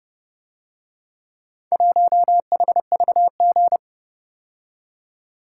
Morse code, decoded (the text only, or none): 1HVG